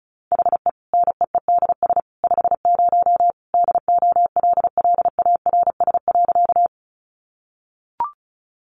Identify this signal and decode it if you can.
Morse code, heard as a keyed tone, about 34 wpm, 700 Hz. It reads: HE NEEDS 50 DOLLARS.
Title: Additional Easy Phrases 35wpm